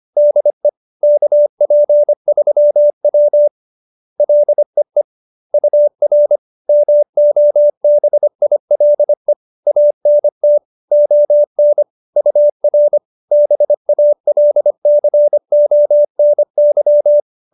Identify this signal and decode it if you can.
Morse code, heard as a keyed tone, about 25 wpm, 590 Hz. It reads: DE KP3W LEE UR MOBILE ANT ON UR BALCONY